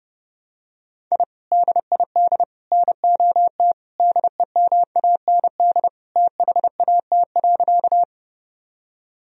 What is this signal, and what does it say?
Morse code, heard as a keyed tone, about 32 wpm, 700 Hz. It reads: I DID NOT DEMAND THAT.
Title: Sentences from Top 1000 Words - Review Part 4 - 30wpm